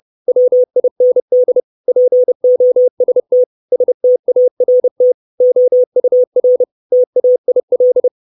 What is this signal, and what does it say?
Morse code, heard as a keyed tone, about 30 wpm, 500 Hz. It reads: WIND POST START OUR TAIL
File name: 632 Words 30WPM